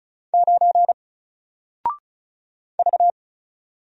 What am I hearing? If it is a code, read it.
Morse code, heard as a keyed tone, about 36 wpm, 700 Hz. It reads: 9 V
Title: Single Character (Letter:Number:Punctuation:Prosign) 35wpm